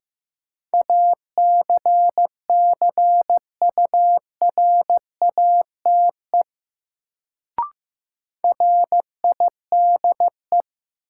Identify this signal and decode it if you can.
Morse code, heard as a keyed tone, about 15 wpm, 700 Hz. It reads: ACCURATE RIDE
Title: Sentences from Top 1200 Words - New Word Drill Alt - 15wpm